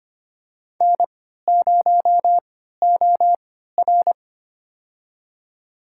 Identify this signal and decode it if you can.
Morse code, heard as a keyed tone, about 25 wpm, spaced as for 14 wpm, 700 Hz. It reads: N0OR
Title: USA Call Signs - 3x Character Spacing - 25wpm